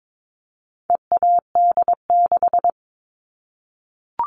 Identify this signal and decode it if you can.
Morse code, heard as a keyed tone, about 22 wpm, 700 Hz. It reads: EAD6
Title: All Cut Numbers - 4-Digit 22wpm